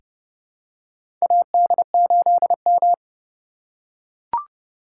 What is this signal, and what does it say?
Morse code, heard as a keyed tone, about 29 wpm, 700 Hz. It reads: AD8M
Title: USA Call Signs - 4 Characters - 3x Character Spacing - 30wpm